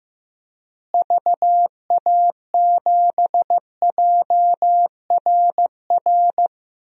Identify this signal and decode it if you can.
Morse code, heard as a keyed tone, about 15 wpm, 700 Hz. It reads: VA7JRR
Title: DXCC-100 Call Signs - 6 Characters - Repeated 3x - 15wpm